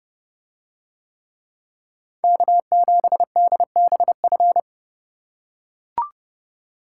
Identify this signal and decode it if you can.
Morse code, heard as a keyed tone, about 29 wpm, 700 Hz. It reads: K7DBF